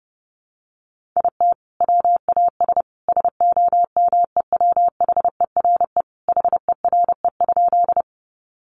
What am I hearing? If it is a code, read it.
Morse code, heard as a keyed tone, about 30 wpm, 700 Hz. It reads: IT WAS SOMEWHERE HERE?
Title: Sentences from Top 1400 Words 30wpm